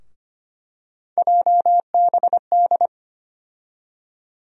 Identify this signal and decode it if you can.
Morse code, heard as a keyed tone, about 25 wpm, 700 Hz. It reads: JBD
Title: USA Call Sign Postfixes - 3 Letters 25wpm